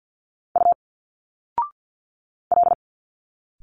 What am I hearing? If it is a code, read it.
Morse code, heard as a keyed tone, about 50 wpm, 700 Hz. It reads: U L
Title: Single Letters 50wpm